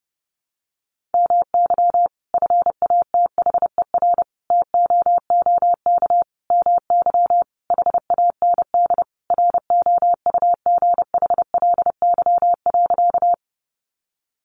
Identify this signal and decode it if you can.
Morse code, heard as a keyed tone, about 30 wpm, 700 Hz. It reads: MY FATHER TOOK MY HAND ROUGHLY.
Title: Sentences from Top 1500 Words 30wpm